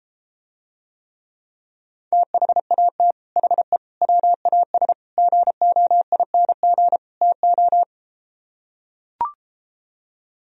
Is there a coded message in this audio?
Yes: Morse code from 2.1 to 7.8 s, heard as a keyed tone, about 33 wpm, 700 Hz. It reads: THAT HE WAS GOING TO